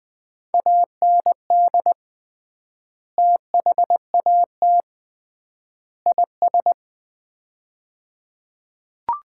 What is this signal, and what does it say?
Morse code, heard as a keyed tone, about 20 wpm, 700 Hz. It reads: AND THAT IS
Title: 3x Word-Spacing - Sets of 3 Words - 20wpm